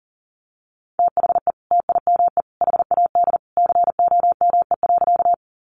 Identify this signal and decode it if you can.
Morse code, heard as a keyed tone, about 40 wpm, 700 Hz. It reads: THE TIME HAD COME.
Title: Sentences from Top 100 Words 40wpm